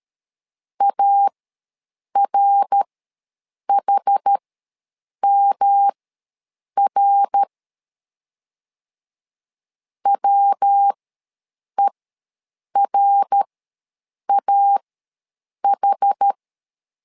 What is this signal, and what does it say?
Morse code, heard as a keyed tone, about 13 wpm, spaced as for 7 wpm, 800 Hz. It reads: ARHMR WERAH
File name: LEZIONI / 3 MWRO / 3 MWR ws 12wpm 1809